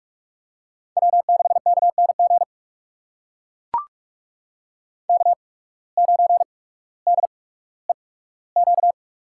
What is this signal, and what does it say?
Morse code, heard as a keyed tone, about 44 wpm, spaced as for 19 wpm, 700 Hz. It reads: W6KNG K9DEQ
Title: USA Call Signs - 5 Characters - 8x Character Spacing - 45wpm